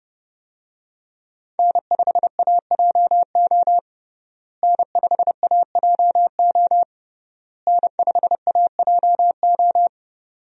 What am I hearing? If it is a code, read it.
Morse code, heard as a keyed tone, about 30 wpm, 700 Hz. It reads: N5AJO N5AJO N5AJO
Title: USA Call Signs - 5 Characters - Repeated 3x - 30wpm